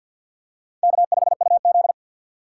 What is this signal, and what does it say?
Morse code, heard as a keyed tone, about 48 wpm, 700 Hz. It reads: K5UZ
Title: USA Call Signs - 4 Characters - 6x Character Spacing - 50wpm